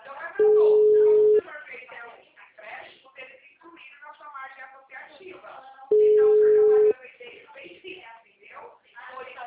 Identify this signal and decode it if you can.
Ringback tone (Europe, 425 Hz), with speech.
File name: out- -2032- - - .59568